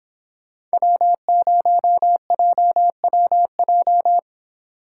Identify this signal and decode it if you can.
Morse code, heard as a keyed tone, about 30 wpm, 700 Hz. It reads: W0JWJ